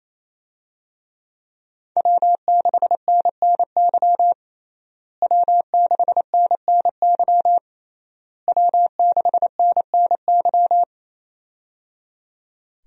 Morse code, heard as a keyed tone, about 28 wpm, 700 Hz. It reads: W6NNY W6NNY W6NNY
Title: USA Call Signs - 5 Characters - Repeated 3x - 28wpm